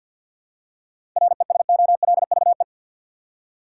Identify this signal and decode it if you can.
Morse code, heard as a keyed tone, about 50 wpm, 700 Hz. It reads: RESOLVE
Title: Seven Letter Words - 50wpm